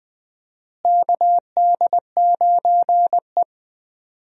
Morse code, heard as a keyed tone, about 20 wpm, 700 Hz. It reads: KD9E
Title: USA Call Signs - 4 Characters - Repeated 3x - 20wpm